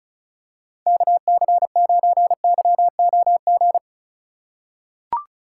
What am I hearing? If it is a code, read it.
Morse code, heard as a keyed tone, about 34 wpm, 700 Hz. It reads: KC9YOG
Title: USA Call Signs - 6 Characters - 2x Character Spacing - 35wpm